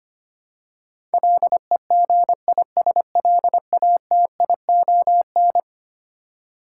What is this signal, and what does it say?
Morse code, heard as a keyed tone, about 25 wpm, 700 Hz. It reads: LEGISLATION
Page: morse-code-ninja-practice-25wpm